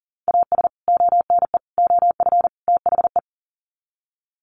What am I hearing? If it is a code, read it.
Morse code, heard as a keyed tone, about 40 wpm, 700 Hz. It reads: AS ONE OF THE
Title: Sets of 4 Words 40wpm